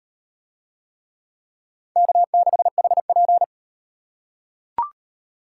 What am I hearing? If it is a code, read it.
Morse code, heard as a keyed tone, about 38 wpm, 700 Hz. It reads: K6HP